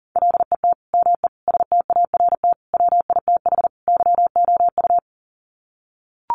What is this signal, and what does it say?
Morse code, heard as a keyed tone, about 40 wpm, 700 Hz. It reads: LET ME START WITH YOU
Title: Sets of 5 Words Encore 40wpm